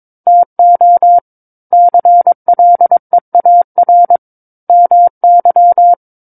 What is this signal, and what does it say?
Morse code, heard as a keyed tone, about 22 wpm, 700 Hz. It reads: TO CLEAR MY
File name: Book One / 13-Chapter 13 - 22 WPM